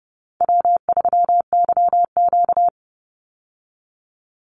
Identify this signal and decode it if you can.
Morse code, heard as a keyed tone, about 28 wpm, 700 Hz. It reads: W3YQ
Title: Easy Call Signs Repeated 3x - 30wpm